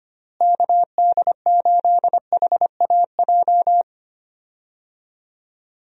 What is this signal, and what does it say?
Morse code, heard as a keyed tone, about 25 wpm, 700 Hz. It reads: KD8HAJ
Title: USA Call Signs - 6 Characters - Repeated 3x - 25wpm